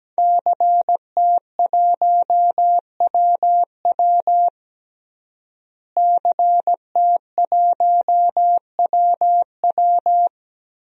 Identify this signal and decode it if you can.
Morse code, heard as a keyed tone, about 17 wpm, 700 Hz. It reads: CT1WW CT1WW
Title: DXCC-100 Call Signs - 5 Characters - Repeated 3x - 17wpm